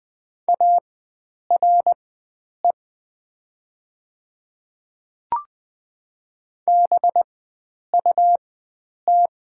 Morse code, heard as a keyed tone, about 20 wpm, spaced as for 9 wpm, 700 Hz. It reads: ARE BUT
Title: 4x Character Spacing - Top 300 Words - 20wpm